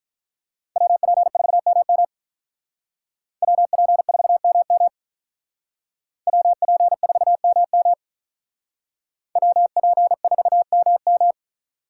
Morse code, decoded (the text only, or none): WP4MM WP4MM WP4MM WP4MM